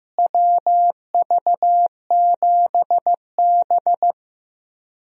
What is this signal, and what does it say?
Morse code, heard as a keyed tone, about 15 wpm, 700 Hz. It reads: WV7B
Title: USA Call Signs - 4 Characters - Repeated 2x - 15wpm